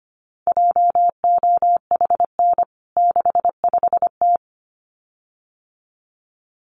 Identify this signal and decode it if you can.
Morse code, heard as a keyed tone, about 25 wpm, 700 Hz. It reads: JOHN 65T
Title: CWT Exchange 25wpm